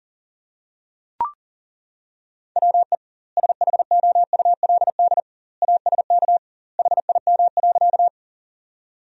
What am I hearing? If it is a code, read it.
Morse code, heard as a keyed tone, about 40 wpm, 700 Hz. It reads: WE SHOULD ASK HIM.
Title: 2x Word-Spacing - Sentences from Top 300 Words - 40wpm